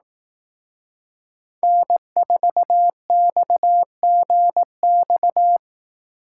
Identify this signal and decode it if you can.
Morse code, heard as a keyed tone, about 18 wpm, 700 Hz. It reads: N4XGX
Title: Speed Racing - USA 1x3 Call Signs 15wpm